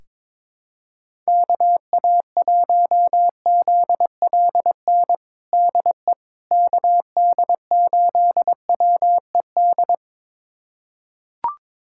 Morse code, heard as a keyed tone, about 21 wpm, 700 Hz. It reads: KA1ZLN DE KD8WED
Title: USA Callsign DE Callsign - 6 Characters - 22wpm